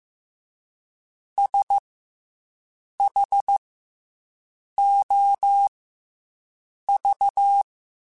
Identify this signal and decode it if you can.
Morse code, heard as a keyed tone, about 15 wpm, spaced as for 6 wpm, 800 Hz. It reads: SHOV